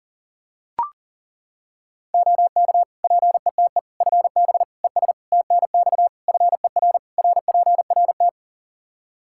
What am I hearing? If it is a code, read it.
Morse code, heard as a keyed tone, about 40 wpm, 700 Hz. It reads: OK PETE FB ES TNX FER RPRT
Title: Practice QSO - Exchange 2 - Info - 40wpm